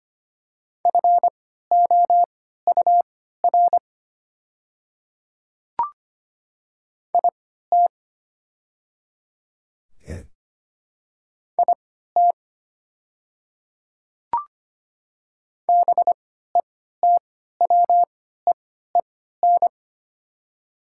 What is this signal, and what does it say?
Morse code, heard as a keyed tone, about 25 wpm, spaced as for 14 wpm, 700 Hz. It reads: FOUR IT IT BETWEEN
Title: 3x Character Spacing - Top 500 Words - 25wpm